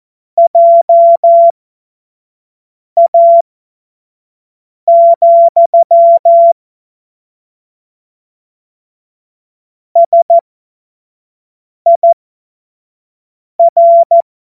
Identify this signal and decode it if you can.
Morse code, heard as a keyed tone, about 14 wpm, spaced as for 5 wpm, 680 Hz. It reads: JA, SIR
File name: Morsekurs / Witze / Witz 30. Mai 2023 25 F70